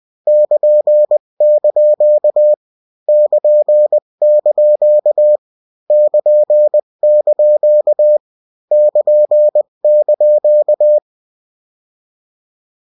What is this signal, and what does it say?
Morse code, heard as a keyed tone, about 20 wpm, 600 Hz. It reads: <KN>) <KN>) <KN>) <KN>)
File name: morse mp3 / lecon16-vitesse 18